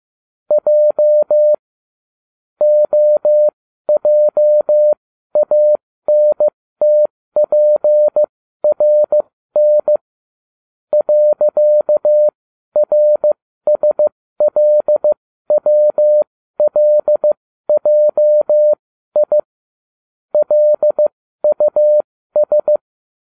Morse code, heard as a keyed tone, about 15 wpm, spaced as for 12 wpm, 600 Hz. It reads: J OJANTPRN .RSLWLJI LUS